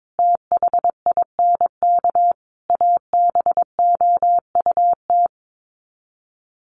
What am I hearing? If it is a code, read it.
Morse code, heard as a keyed tone, about 22 wpm, 700 Hz. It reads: THINK ABOUT
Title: Sets of 5 Words 22wpm